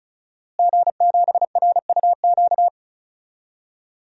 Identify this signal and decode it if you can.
Morse code, heard as a keyed tone, about 34 wpm, 700 Hz. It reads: G7RUQ
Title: DXCC-100 Call Signs - 5 Characters - Repeated 2x - 35wpm